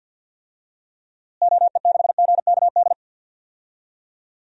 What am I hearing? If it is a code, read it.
Morse code, heard as a keyed tone, about 50 wpm, 700 Hz. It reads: OE6GKD
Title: DXCC-100 Call Signs - 6 Characters - Repeated 2x - 50wpm